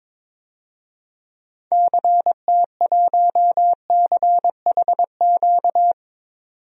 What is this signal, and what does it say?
Morse code, heard as a keyed tone, about 21 wpm, 700 Hz. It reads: CT1CHQ
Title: DXCC-100 Call Signs - 6 Characters - 22wpm